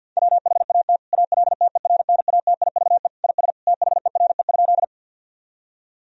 Morse code, heard as a keyed tone, about 50 wpm, 700 Hz. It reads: WHAT ALTERNATIVE IS THERE?
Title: Sentences from Top 1000 Words - Review Part 4 - 50wpm